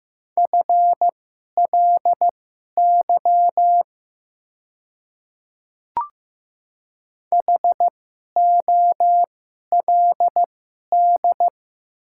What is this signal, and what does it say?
Morse code, heard as a keyed tone, about 15 wpm, spaced as for 11 wpm, 700 Hz. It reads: FLY HOLD
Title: 2x Character Spacing - Top 500 Words - 15wpm